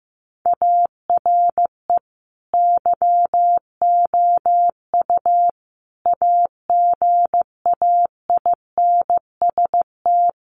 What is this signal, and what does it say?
Morse code, heard as a keyed tone, about 15 wpm, 700 Hz. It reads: ARE YOU AGAINST
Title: Sentences from Top 1400 Words 15wpm